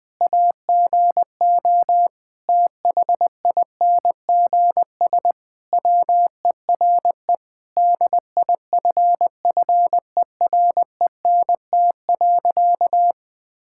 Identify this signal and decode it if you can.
Morse code, heard as a keyed tone, about 21 wpm, 700 Hz. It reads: AGO THINGS WERE DIFFERENT.
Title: Sentences with 4-Word Phrases - Spoken Beginning - 20wpm